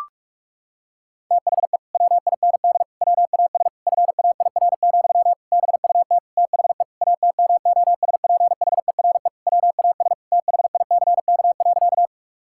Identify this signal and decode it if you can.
Morse code, heard as a keyed tone, about 42 wpm, 700 Hz. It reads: THE WIND WAS FAIR, BUT THE ATMOSPHERE WAS THICK.
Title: Sentences from Top 1600 Words 45wpm